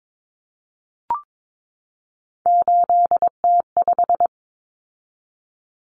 Morse code, heard as a keyed tone, about 22 wpm, 700 Hz. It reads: 8T5
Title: Common Cut Numbers - 3-Digit 22wpm